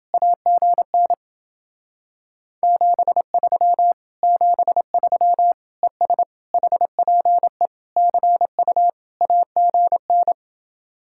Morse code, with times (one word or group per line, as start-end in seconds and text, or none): 0.1-1.1 AGN
2.6-3.9 73
4.2-5.5 73
5.8-6.2 ES
6.5-7.7 HPE
8.0-8.9 CU
9.2-10.3 AGN